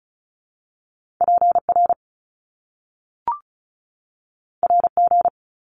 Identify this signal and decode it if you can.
Morse code, heard as a keyed tone, about 36 wpm, 700 Hz. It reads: PR RG
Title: Top Two Letter Combinations in Words 35wpm